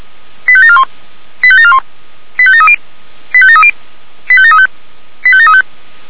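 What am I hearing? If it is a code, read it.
Five-tone selective call (ZVEI), heard as quick runs of five tones, tones of about 70 ms. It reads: 85741 85741 85730 85730 85735 85735
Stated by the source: ZVEI five-tone call 85745 received on an SDR.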